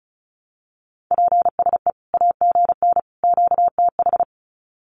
Morse code, heard as a keyed tone, about 34 wpm, 700 Hz. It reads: PSE AGN QTH
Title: Common QSO Phrases 35wpm